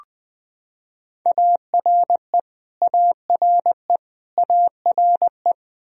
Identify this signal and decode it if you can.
Morse code, heard as a keyed tone, about 20 wpm, 700 Hz. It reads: ARE ARE ARE
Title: Top 100 Words Repeated 20wpm